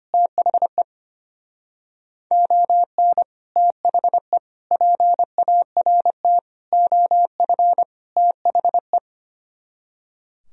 Morse code, heard as a keyed tone, about 30 wpm, 700 Hz. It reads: THE ON THE PART OF THE
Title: Speed-Racing - Sets of 5 Words 25wpm